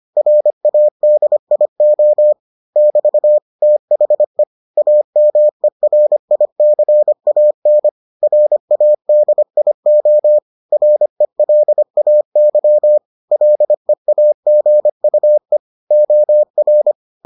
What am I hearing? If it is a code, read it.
Morse code, heard as a keyed tone, about 25 wpm, 600 Hz. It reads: RADIO <BT> THE AMERICAN RADIO RELAY LEAGUE OR